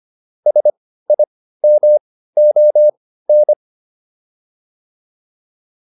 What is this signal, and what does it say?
Morse code, heard as a keyed tone, about 25 wpm, spaced as for 15 wpm, 600 Hz. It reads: SIMON